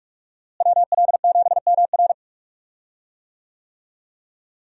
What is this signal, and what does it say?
Morse code, heard as a keyed tone, about 41 wpm, 700 Hz. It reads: WL7MR
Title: USA Call Signs - 5 Characters - Repeated 3x - 45wpm